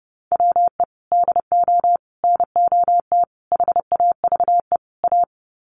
Morse code, heard as a keyed tone, about 30 wpm, 700 Hz. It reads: WE DO NOT HAVE A
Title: Sets of 5 Words 30wpm